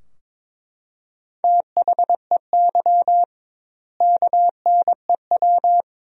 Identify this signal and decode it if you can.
Morse code, heard as a keyed tone, about 21 wpm, 700 Hz. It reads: THEY KNEW
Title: 2x Word-Spacing - Sentences from Top 400 Words - 22wpm